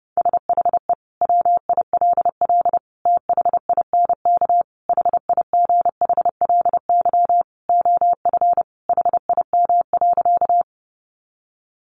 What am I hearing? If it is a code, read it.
Morse code, heard as a keyed tone, about 30 wpm, 700 Hz. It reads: SHE WILL THINK HIGHLY OF HIM.
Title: Sentences from Top 1100 Words 30wpm